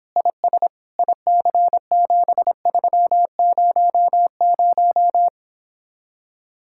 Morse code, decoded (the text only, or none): IS IC7300